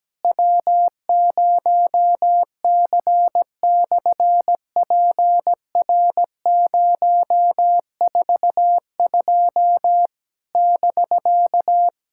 Morse code, heard as a keyed tone, about 17 wpm, 700 Hz. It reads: W0C/PR042 <BK>
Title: SOTA QSO Practice 17wpm